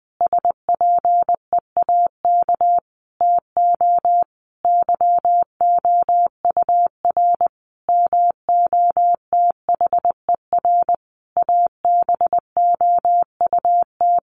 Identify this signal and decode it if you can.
Morse code, heard as a keyed tone, about 20 wpm, 700 Hz. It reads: SPEAK TO YOUR MOTHER ABOUT
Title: Sentences from Top 1100 Words 20wpm